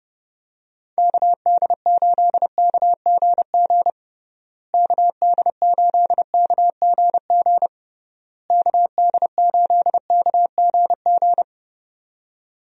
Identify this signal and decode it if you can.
Morse code, heard as a keyed tone, about 30 wpm, 700 Hz. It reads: KD8KGG KD8KGG KD8KGG
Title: USA Call Signs - 6 Characters - Repeated 3x - 30wpm